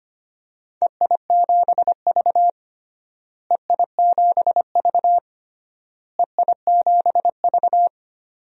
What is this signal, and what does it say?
Morse code, heard as a keyed tone, about 25 wpm, 700 Hz. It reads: EI7V EI7V EI7V
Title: DXCC-100 Call Signs - 4 Characters - Repeated 3x - 25wpm